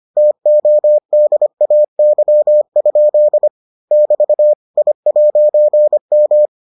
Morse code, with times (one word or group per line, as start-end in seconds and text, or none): 0.2-3.5 TODAY?
3.9-4.5 <BT>
4.8-6.5 I'M